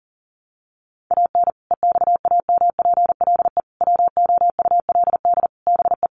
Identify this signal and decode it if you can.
Morse code, heard as a keyed tone, about 40 wpm, 700 Hz. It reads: AN EXAMPLE WOULD BE
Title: Sentences from Top 400 Words 40wpm